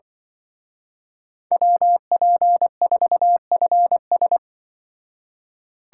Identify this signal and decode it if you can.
Morse code, heard as a keyed tone, about 23 wpm, 700 Hz. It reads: WP4FS
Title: Speed Racing - USA 2x2 Call Signs 20wpm